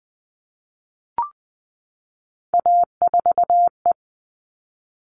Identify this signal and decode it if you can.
Morse code, heard as a keyed tone, about 20 wpm, 700 Hz. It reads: A4E
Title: All Cut Numbers - 3-Digit 20wpm